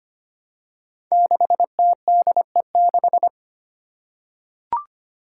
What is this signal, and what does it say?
Morse code, heard as a keyed tone, about 25 wpm, 700 Hz. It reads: 6TDE6
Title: Five Challenging Letter-Number Combinations 25wpm